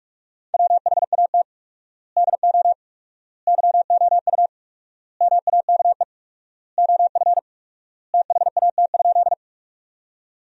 Morse code, heard as a keyed tone, about 46 wpm, 700 Hz. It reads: WHAT DO YOU MAKE OF THAT?
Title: 4x Word-Spacing - Sentences from Top 100 Words - 45wpm